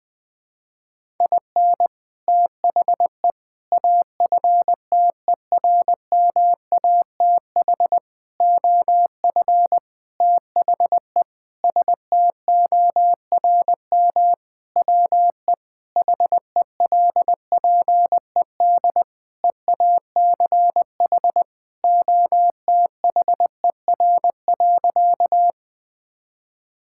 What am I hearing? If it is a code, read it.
Morse code, heard as a keyed tone, about 20 wpm, 700 Hz. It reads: IN THE AFTERMATH OF THE STORM WE HELPED EACH OTHER.
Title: Sentences with 5-Word Phrases - Spoken Beginning - 20wpm